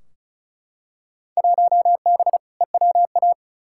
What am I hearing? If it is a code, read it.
Morse code, heard as a keyed tone, about 36 wpm, 700 Hz. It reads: 1B EWA
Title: Field Day Exchanges Encore - 35wpm